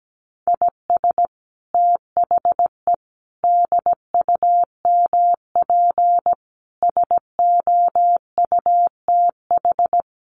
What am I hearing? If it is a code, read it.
Morse code, heard as a keyed tone, about 17 wpm, 700 Hz. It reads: IS THE DUMP SOUTH
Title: Sentences from Top 1300 Words 17wpm